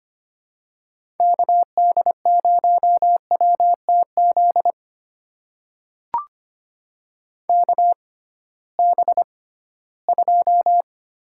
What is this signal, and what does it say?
Morse code, heard as a keyed tone, about 25 wpm, spaced as for 16 wpm, 700 Hz. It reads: KD0WTZ KB2
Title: USA Call Signs - 6 Characters - 6x Character Spacing - 25wpm